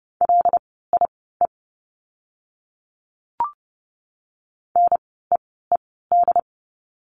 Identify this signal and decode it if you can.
Morse code, heard as a keyed tone, about 30 wpm, spaced as for 17 wpm, 700 Hz. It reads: LIE NEED
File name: Lesson 16f - Letter c - Words 30wpm